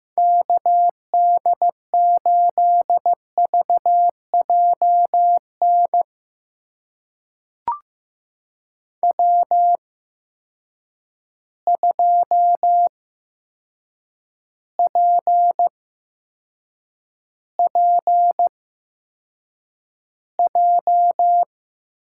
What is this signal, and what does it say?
Morse code, heard as a keyed tone, about 15 wpm, spaced as for 7 wpm, 700 Hz. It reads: KD8VJN W2PPJ
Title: USA Call Signs - 8x Character Spacing - 15wpm